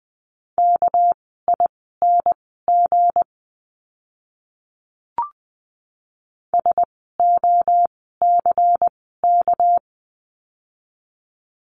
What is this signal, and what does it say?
Morse code, heard as a keyed tone, about 20 wpm, spaced as for 15 wpm, 700 Hz. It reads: KING SOCK
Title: 199 Lesson 32f - Letter k - Words 20wpm